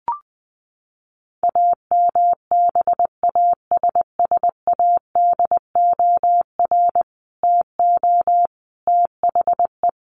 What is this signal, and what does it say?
Morse code, heard as a keyed tone, about 20 wpm, 700 Hz. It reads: AMBASSADOR TO THE
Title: Sets of 5 Words Encore2 20wpm